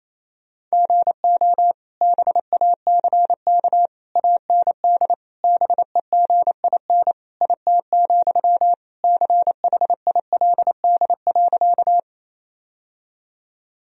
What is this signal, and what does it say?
Morse code, heard as a keyed tone, about 28 wpm, 700 Hz. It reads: GO BACK AND BEGIN IT, CHILD.
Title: Sentences from Top 1000 Words - Review Part 1 - 28wpm